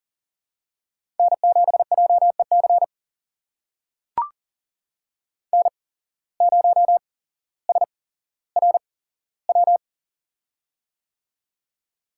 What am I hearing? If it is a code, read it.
Morse code, heard as a keyed tone, about 41 wpm, spaced as for 17 wpm, 700 Hz. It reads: N7JEC N0SRW
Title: USA Call Signs - 5 Characters - 8x Character Spacing - 40wpm